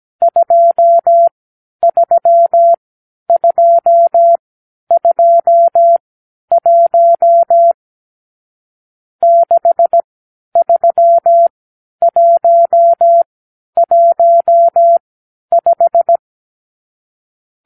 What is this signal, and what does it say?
Morse code, heard as a keyed tone, about 17 wpm, spaced as for 10 wpm, 680 Hz. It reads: 23221 63115
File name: k7qo / Lesson 50 (1-6)